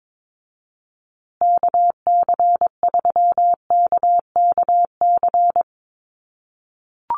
Morse code, heard as a keyed tone, about 22 wpm, 700 Hz. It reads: KC3KKC
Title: USA Call Signs Repeated 3x 22wpm